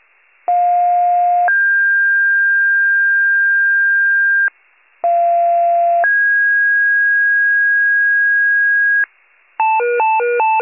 Two-tone page, heard as a one-second tone, then a three-second tone, about 710 then 1600 Hz, about 690 then 1650 Hz. It is heard from 0.5 to 9.0 s.